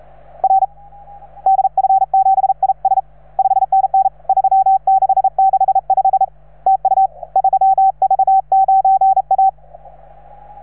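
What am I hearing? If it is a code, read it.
Morse code, heard as a keyed tone, about 37 wpm, 770 Hz. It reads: R DF7IS 5NN 3665 TU 3V9A
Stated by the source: Morse from station 3V9A.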